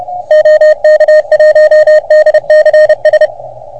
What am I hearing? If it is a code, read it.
Morse code, heard as a keyed tone, about 30 wpm, 610 Hz. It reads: OK1DCS
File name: Archiv / CUC381 / OK1FGD CUC 381